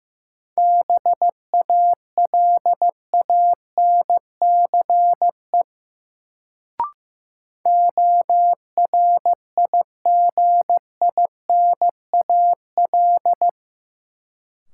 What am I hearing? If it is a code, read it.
Morse code, heard as a keyed tone, about 15 wpm, 700 Hz. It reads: BALANCE ORIGINAL
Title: Sentences from Top 900 Words - New Word Drill - 15wpm